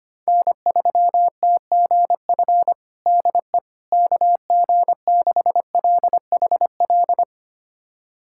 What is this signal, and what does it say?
Morse code, heard as a keyed tone, about 25 wpm, 700 Hz. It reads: N3TGF DE KG6LHL
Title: USA Callsign DE Callsign 25wpm